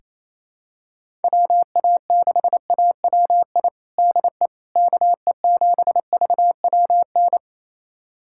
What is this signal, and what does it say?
Morse code, heard as a keyed tone, about 27 wpm, 700 Hz. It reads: WA6AWI DE KE7VWN
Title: USA Callsign DE Callsign - 6 Characters - 28wpm